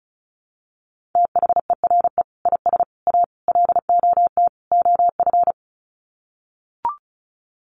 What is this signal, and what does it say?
Morse code, heard as a keyed tone, about 34 wpm, 700 Hz. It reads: THERE IS A LOT OF
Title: Sets of 5 Words 35wpm